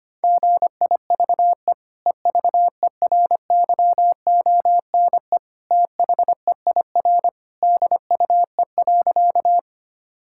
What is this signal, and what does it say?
Morse code, heard as a keyed tone, about 25 wpm, 700 Hz. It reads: GIVE EVERYONE THEIR DUE.
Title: Sentences from Top 1000 Words - Review Part 6 - 25wpm